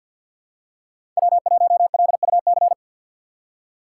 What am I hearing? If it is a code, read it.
Morse code, heard as a keyed tone, about 52 wpm, 700 Hz. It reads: W1LUC
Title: USA Call Signs - 5 Characters - 3x Character Spacing - 50wpm